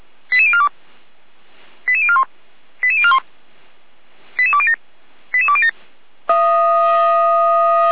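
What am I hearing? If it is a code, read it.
Five-tone selective call (ZVEI), heard as quick runs of five tones, tones of about 70 ms. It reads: 88042 88041 88041 88207 88207
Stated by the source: Five-tone ZVEI selective call received on an SDR.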